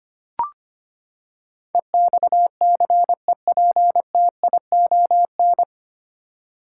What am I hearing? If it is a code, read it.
Morse code, heard as a keyed tone, about 25 wpm, 700 Hz. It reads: EXCEPTION
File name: Nine Letter Words - 25wpm